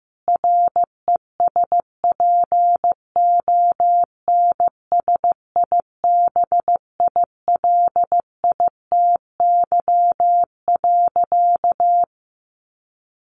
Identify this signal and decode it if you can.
Morse code, heard as a keyed tone, about 15 wpm, 700 Hz. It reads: RESPONSIBILITY.
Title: Sentences from Top 1200 Words 15wpm